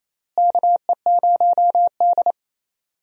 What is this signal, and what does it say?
Morse code, heard as a keyed tone, about 28 wpm, 700 Hz. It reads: KE0D
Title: USA Call Signs - 4 Characters - Repeated 3x - 28wpm